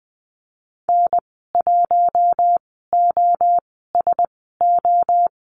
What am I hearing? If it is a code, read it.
Morse code, heard as a keyed tone, about 20 wpm, spaced as for 14 wpm, 700 Hz. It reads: N1OSO
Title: 042 Lesson 8g - Number 1 - Call Signs 20wpm